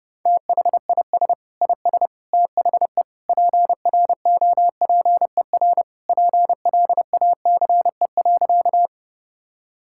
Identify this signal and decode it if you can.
Morse code, heard as a keyed tone, about 29 wpm, 700 Hz. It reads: THIS IS THE PROPER PLACE.
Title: Sentences from Top 1000 Words - Review Part 5 - 30wpm